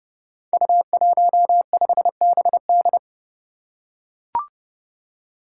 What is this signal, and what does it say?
Morse code, heard as a keyed tone, about 30 wpm, 700 Hz. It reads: U15BD
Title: Five Challenging Letter-Number Combinations 30wpm